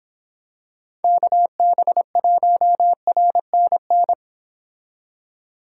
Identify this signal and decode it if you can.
Morse code, heard as a keyed tone, about 26 wpm, 700 Hz. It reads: KB1RNN